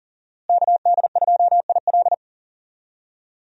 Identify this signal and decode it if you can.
Morse code, heard as a keyed tone, about 40 wpm, 700 Hz. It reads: KD2IL
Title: USA Call Signs - 5 Characters - Repeated 2x - 40wpm